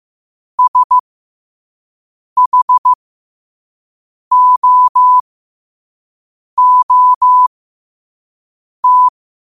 Morse code, heard as a keyed tone, about 15 wpm, spaced as for 5 wpm, 1000 Hz. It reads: SHOOT
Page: New quotes every day in morse code at 5 Words per minute.